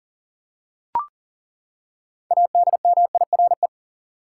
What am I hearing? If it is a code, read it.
Morse code, heard as a keyed tone, about 40 wpm, 700 Hz. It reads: ADMIRE
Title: Six Letter Words - 40wpm